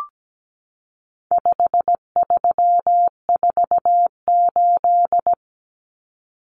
Morse code, heard as a keyed tone, about 17 wpm, 700 Hz. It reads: 5348
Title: Common Cut Numbers - 4-Digit 17wpm